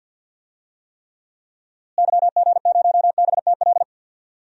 Morse code, heard as a keyed tone, about 47 wpm, 700 Hz. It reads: YG0BTL
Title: DXCC-100 Call Signs - 6 Characters - Repeated 2x - 50wpm